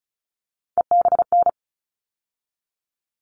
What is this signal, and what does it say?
Morse code, heard as a keyed tone, about 34 wpm, 700 Hz. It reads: EBN
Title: All Cut Numbers - 3-Digit 35wpm